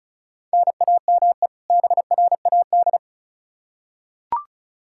Morse code, heard as a keyed tone, about 34 wpm, 700 Hz. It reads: NAME BRAD
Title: Practice QSO - Exchange 1 - Beginning - 35wpm